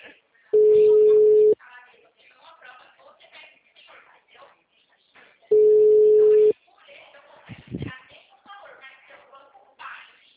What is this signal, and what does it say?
Ringback tone (Europe, 425 Hz), with speech.
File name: out- -2018- - - .14